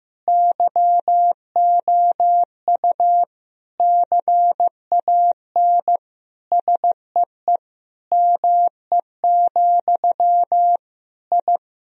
Morse code, heard as a keyed tone, about 14 wpm, 700 Hz. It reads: YOU CAN SEE ME, I
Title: Sentences from Top 1000 Words - Review Part 3 - 15wpm